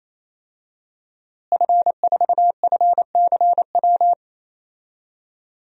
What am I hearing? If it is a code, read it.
Morse code, heard as a keyed tone, about 28 wpm, 700 Hz. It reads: F4FCW
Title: DXCC-100 Call Signs - 5 Characters - Repeated 3x - 28wpm